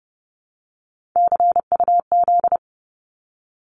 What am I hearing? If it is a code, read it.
Morse code, heard as a keyed tone, about 30 wpm, 700 Hz. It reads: CUZ
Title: Abbreviations 30wpm